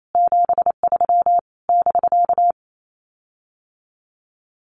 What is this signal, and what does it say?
Morse code, heard as a keyed tone, about 28 wpm, 700 Hz. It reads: 73 <BK>
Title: Common QSO Phrases Encore 28wpm